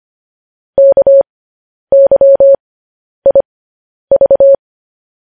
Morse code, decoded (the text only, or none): KYIV